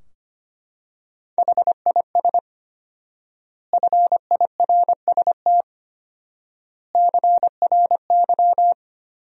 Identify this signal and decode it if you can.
Morse code, heard as a keyed tone, about 25 wpm, 700 Hz. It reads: HIS FIRST CRY
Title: 4x Word-Spacing - Sentences from Top 500 Words - 25wpm